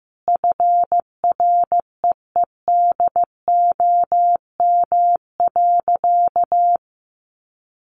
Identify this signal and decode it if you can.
Morse code, heard as a keyed tone, about 15 wpm, 700 Hz. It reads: FREEDOM.
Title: Sentences from Top 800 Words 15wpm